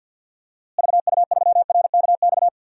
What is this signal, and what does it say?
Morse code, heard as a keyed tone, about 52 wpm, 700 Hz. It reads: VU3RKX
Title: DXCC-100 Call Signs - 6 Characters - Repeated 3x - 50wpm